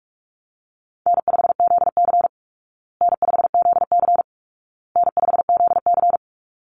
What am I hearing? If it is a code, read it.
Morse code, heard as a keyed tone, about 44 wpm, 700 Hz. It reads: N5ZC N5ZC N5ZC
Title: Easy Call Signs Repeated 3x - 45wpm